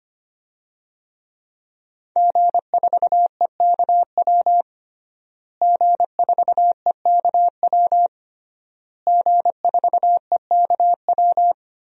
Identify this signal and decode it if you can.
Morse code, heard as a keyed tone, about 25 wpm, 700 Hz. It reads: G4EKW G4EKW G4EKW
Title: DXCC-100 Call Signs - 5 Characters - Repeated 3x - 25wpm